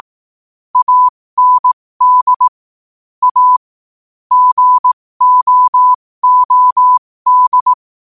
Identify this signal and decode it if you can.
Morse code, heard as a keyed tone, about 18 wpm, spaced as for 15 wpm, 1000 Hz. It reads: AND A GOOD
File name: qotdfiles / 15wpm